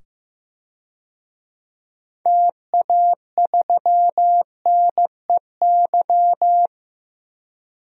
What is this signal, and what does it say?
Morse code, heard as a keyed tone, about 15 wpm, 700 Hz. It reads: TA3NEY
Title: DXCC-100 Call Signs - 6 Characters - 15wpm